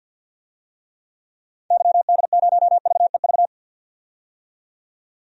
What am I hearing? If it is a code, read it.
Morse code, heard as a keyed tone, about 53 wpm, 700 Hz. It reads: YD0VEV
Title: DXCC-100 Call Signs - 6 Characters - 50wpm